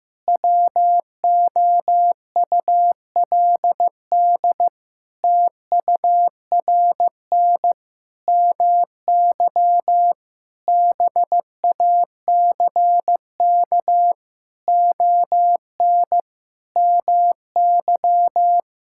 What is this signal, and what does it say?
Morse code, heard as a keyed tone, about 15 wpm, 700 Hz. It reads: WOULD TURN MY BACK ON MY
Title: Sentences from Top 1000 Words - Review Part 6 - 15wpm